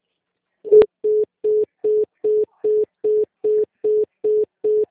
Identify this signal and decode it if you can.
Congestion tone (425 Hz).
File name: out- -2027- - - .24895